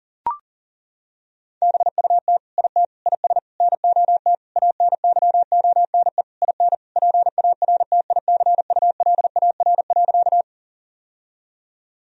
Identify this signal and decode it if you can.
Morse code, heard as a keyed tone, about 40 wpm, 700 Hz. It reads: BUT IT IS NOT ANYONE IN PARTICULAR.
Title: Sentences from Top 1000 Words - Review Part 4 - 40wpm